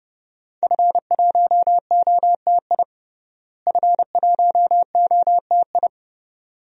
Morse code, heard as a keyed tone, about 32 wpm, 700 Hz. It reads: F1OTI F1OTI
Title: DXCC-100 Call Signs - 5 Characters - Repeated 2x - 30wpm